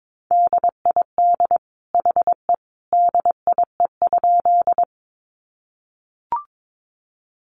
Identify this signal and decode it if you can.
Morse code, heard as a keyed tone, about 22 wpm, 700 Hz. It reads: DID HE DIE?
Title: Sentences from Top 1300 Words 22wpm